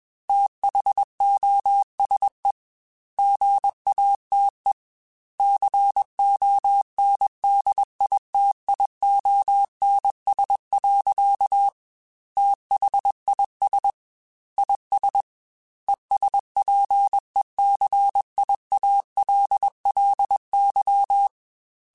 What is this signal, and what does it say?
Morse code, heard as a keyed tone, about 21 wpm, 800 Hz. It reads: THOSE GATE CONDITIONS. THIS IS ESPECIALLY
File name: Technical Subjects / Technical Subjects-mp3s / AND Gate @20